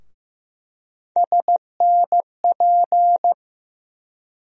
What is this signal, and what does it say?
Morse code, heard as a keyed tone, about 15 wpm, 700 Hz. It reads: SNP
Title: USA Call Sign Postfixes - 3 Letters 15wpm